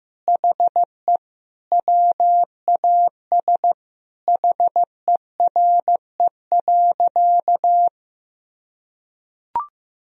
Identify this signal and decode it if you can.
Morse code, heard as a keyed tone, about 15 wpm, 700 Hz. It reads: HE WAS HERE.
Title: Sentences from Top 1000 Words - Review Part 3 - 15wpm